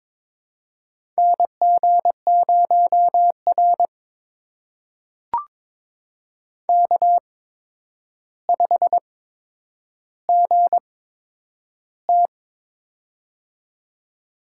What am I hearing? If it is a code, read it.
Morse code, heard as a keyed tone, about 23 wpm, spaced as for 10 wpm, 700 Hz. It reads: NG0R K5GT
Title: USA Call Signs - 4 Characters - 8x Character Spacing - 22wpm